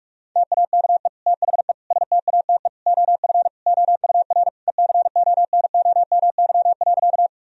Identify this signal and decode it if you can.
Morse code, heard as a keyed tone, about 41 wpm, 700 Hz. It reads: TAKE THE STATE OF OUR ECONOMY.
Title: Sentences from Top 1000 Words - Review Part 3 - 45wpm